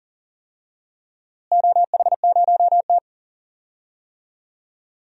Morse code, heard as a keyed tone, about 40 wpm, 700 Hz. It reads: OH0T
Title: DXCC-100 Call Signs - 4 Characters - Repeated 2x - 40wpm